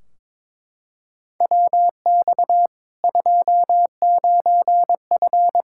Morse code, heard as a keyed tone, about 21 wpm, 700 Hz. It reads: WX 29F
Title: Practice QSO - Exchange 2 - Info - 22wpm